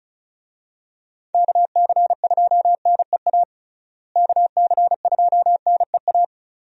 Morse code, heard as a keyed tone, about 34 wpm, 700 Hz. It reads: KC2NEA KC2NEA
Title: USA Call Signs - 6 Characters - Repeated 2x - 35wpm